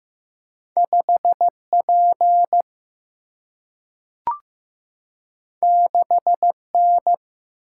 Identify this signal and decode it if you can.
Morse code, heard as a keyed tone, about 15 wpm, 700 Hz. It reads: 5P 6N
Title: Single-Letter Single-Number 15wpm